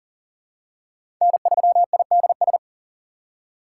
Morse code, heard as a keyed tone, about 40 wpm, 700 Hz. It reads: N3IDS
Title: USA Call Signs - 3x Character Spacing - 40wpm